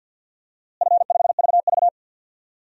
Morse code, heard as a keyed tone, about 51 wpm, 700 Hz. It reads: F5VV
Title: DXCC-100 Call Signs - 4 Characters - Repeated 3x - 50wpm